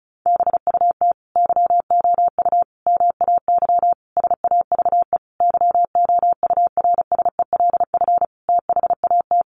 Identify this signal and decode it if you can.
Morse code, heard as a keyed tone, about 34 wpm, 700 Hz. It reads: BUT YOU MAY SAVE YOURSELF THAT
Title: Sentences from Top 1100 Words 35wpm